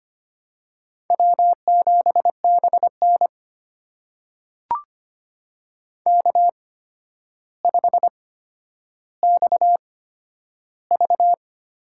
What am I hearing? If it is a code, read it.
Morse code, heard as a keyed tone, about 25 wpm, spaced as for 11 wpm, 700 Hz. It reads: W7BN K5XV
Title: USA Call Signs - 8x Character Spacing - 25wpm